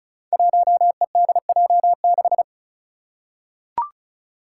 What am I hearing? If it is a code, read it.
Morse code, heard as a keyed tone, about 36 wpm, 700 Hz. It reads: 1EDJ6
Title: Five Challenging Letter-Number Combinations 35wpm